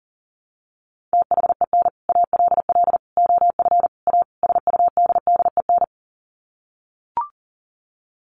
Morse code, heard as a keyed tone, about 40 wpm, 700 Hz. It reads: THEN ALL OF A SUDDEN
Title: Sets of 5 Words Encore 40wpm